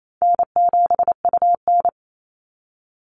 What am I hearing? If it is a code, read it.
Morse code, heard as a keyed tone, about 28 wpm, 700 Hz. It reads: N7UN
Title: NA SOTA Activator Callsigns 28wpm